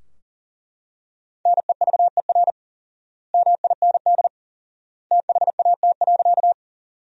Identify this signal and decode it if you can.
Morse code, heard as a keyed tone, about 40 wpm, 700 Hz. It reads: NEVER MIND THAT.
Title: 4x Word-Spacing - Sentences from Top 500 Words - 40wpm